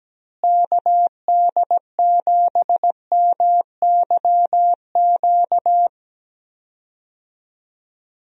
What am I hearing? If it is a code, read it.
Morse code, heard as a keyed tone, about 17 wpm, 700 Hz. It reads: KD7MYQ
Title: USA Call Signs - 6 Characters - Repeated 3x - 17wpm